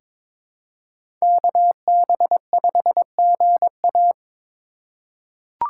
Morse code, heard as a keyed tone, about 22 wpm, 700 Hz. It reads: KB5GA
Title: K1USN SST QSO Practice - 22wpm